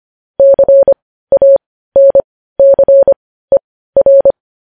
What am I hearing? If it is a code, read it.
Morse code, heard as a keyed tone, about 25 wpm, spaced as for 15 wpm, 550 Hz. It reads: CANCER
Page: news-headlines-in-morse-code-at-15-wpm